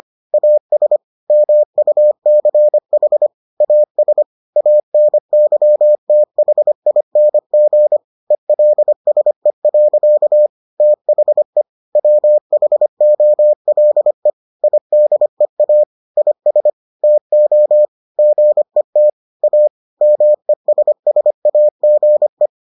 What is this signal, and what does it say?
Morse code, heard as a keyed tone, about 25 wpm, 600 Hz. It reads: AS MUCH AS ANYTHING ELSE. THE WHOLE IDEA IS TO GET A MESSAGE